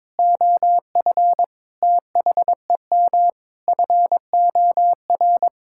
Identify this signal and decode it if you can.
Morse code, heard as a keyed tone, about 21 wpm, 700 Hz. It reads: OF THEM FOR
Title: Sentences from Top 1000 Words - Review Part 5 - 22wpm